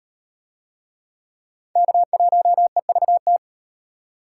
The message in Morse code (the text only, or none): K1EVT